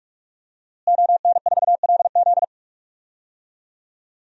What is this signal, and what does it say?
Morse code, heard as a keyed tone, about 46 wpm, 700 Hz. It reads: ON4LZ
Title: DXCC-100 Call Signs - 5 Characters - Repeated 3x - 45wpm